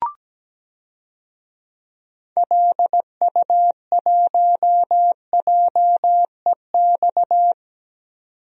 Morse code, heard as a keyed tone, about 17 wpm, 700 Hz. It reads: LU1JEX
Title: DXCC-100 Call Signs - 6 Characters - 17wpm